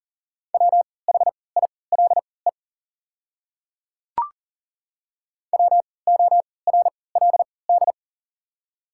Morse code, heard as a keyed tone, about 40 wpm, spaced as for 23 wpm, 700 Hz. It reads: WHILE WORLD
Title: 3x Character Spacing - Top 300 Words - 40wpm